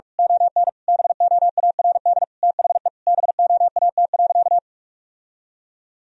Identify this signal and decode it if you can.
Morse code, heard as a keyed tone, about 46 wpm, 700 Hz. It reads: ON BOARD THE BOAT.
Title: Sentences from Top 1000 Words - Review Part 4 - 45wpm